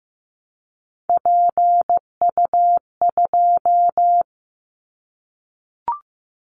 Morse code, heard as a keyed tone, about 15 wpm, 700 Hz. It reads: PU2
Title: Callsign Prefixes for Top 50 DXCC - 15wpm